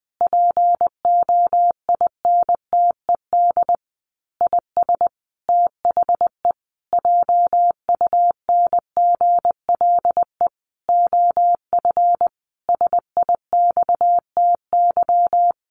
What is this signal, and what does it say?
Morse code, heard as a keyed tone, about 20 wpm, 700 Hz. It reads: POINTED IS THE JUNGLE OF SIXTY
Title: A Sound Of Thunder 20wpm